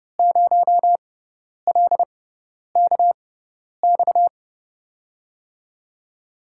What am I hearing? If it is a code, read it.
Morse code, heard as a keyed tone, about 30 wpm, spaced as for 10 wpm, 700 Hz. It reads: 0LKX
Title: USA Call Signs - 5 Characters - 6x Character Spacing - 30wpm